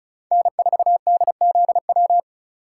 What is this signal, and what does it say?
Morse code, heard as a keyed tone, about 34 wpm, 700 Hz. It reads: N4DZW
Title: USA Call Signs - 3x Character Spacing - 35wpm